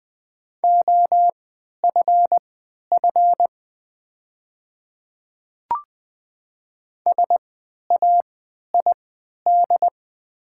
Morse code, heard as a keyed tone, about 20 wpm, spaced as for 11 wpm, 700 Hz. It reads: OFF SAID
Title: 3x Character Spacing - Top 300 Words - 20wpm